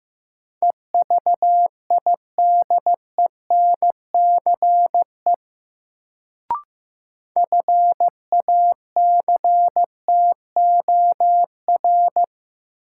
Morse code, heard as a keyed tone, about 15 wpm, 700 Hz. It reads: EVIDENCE FACTOR
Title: Sentences from Top 1300 Words - New Word Drill - 15wpm